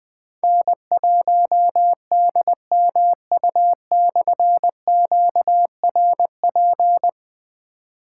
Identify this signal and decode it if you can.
Morse code, heard as a keyed tone, about 20 wpm, 700 Hz. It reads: N1DMU/QRP
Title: Difficult Call Signs 20wpm